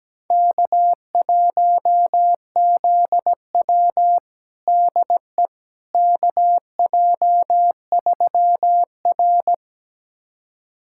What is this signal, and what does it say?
Morse code, heard as a keyed tone, about 17 wpm, 700 Hz. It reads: K1ZW DE KJ3R
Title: USA Callsign DE Callsign - 4 Characters - 17wpm